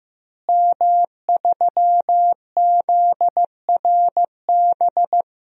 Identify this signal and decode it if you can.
Morse code, heard as a keyed tone, about 15 wpm, 700 Hz. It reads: M3ZRB
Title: DXCC-100 Call Signs - 5 Characters - Repeated 3x - 15wpm